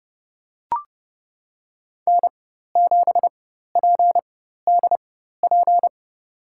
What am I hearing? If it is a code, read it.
Morse code, heard as a keyed tone, about 32 wpm, spaced as for 14 wpm, 700 Hz. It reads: N7PDP
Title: USA Call Signs - 5 Characters - 4x Character Spacing - 30wpm